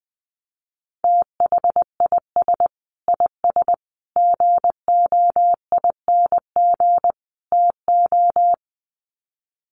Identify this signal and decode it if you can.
Morse code, heard as a keyed tone, about 20 wpm, 700 Hz. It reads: THIS IS GOING TO
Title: Sets of 4 Words 20wpm